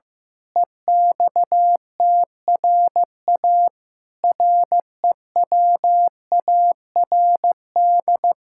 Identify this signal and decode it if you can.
Morse code, heard as a keyed tone, about 15 wpm, 700 Hz. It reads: EXTRA REWARD
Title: Pangram Challenge v2 - 15wpm